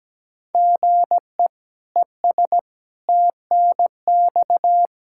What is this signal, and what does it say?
Morse code, heard as a keyed tone, about 17 wpm, 700 Hz. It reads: GE ES TNX
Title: K1USN SST QSO Practice 17wpm